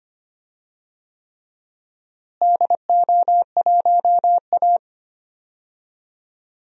Morse code, heard as a keyed tone, about 25 wpm, 700 Hz. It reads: DO1A
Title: DXCC-100 Call Signs - 4 Characters - Repeated 3x - 25wpm